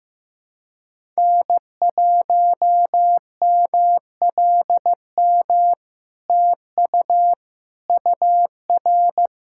Morse code, heard as a keyed tone, about 15 wpm, 700 Hz. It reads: N1MLM TU UR
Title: SOTA QSO Practice 15wpm